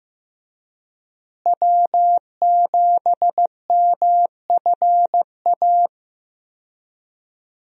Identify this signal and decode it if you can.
Morse code, heard as a keyed tone, about 15 wpm, 700 Hz. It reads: W7MFA
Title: USA Call Signs - 5 Characters - Repeated 2x - 15wpm